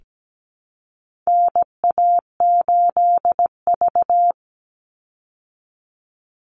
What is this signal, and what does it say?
Morse code, heard as a keyed tone, about 17 wpm, 700 Hz. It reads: NA8V
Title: Call Signs Repeated 3x - 17wpm